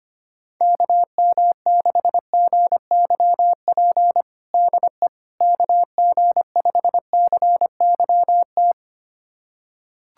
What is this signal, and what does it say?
Morse code, heard as a keyed tone, about 25 wpm, 700 Hz. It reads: KM6GYP DE KG5CYT
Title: USA Callsign DE Callsign - 6 Characters - 25wpm